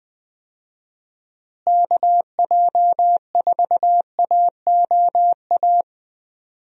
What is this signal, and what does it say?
Morse code, heard as a keyed tone, about 20 wpm, 700 Hz. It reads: KJ4AOA
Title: USA Call Signs - 6 Characters - Repeated 2x - 20wpm